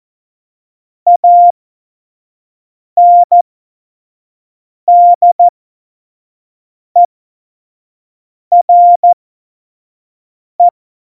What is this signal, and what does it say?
Morse code, heard as a keyed tone, about 14 wpm, spaced as for 5 wpm, 700 Hz. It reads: ANDERE
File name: Morsekurs / Witze / Witz 26. Juni 2018, 25 F70